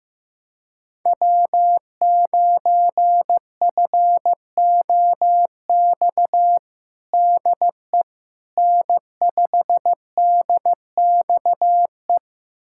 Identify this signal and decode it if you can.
Morse code, heard as a keyed tone, about 15 wpm, 700 Hz. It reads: W9FOX DE N5DXE
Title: USA Callsign DE Callsign - 5 Characters - 15wpm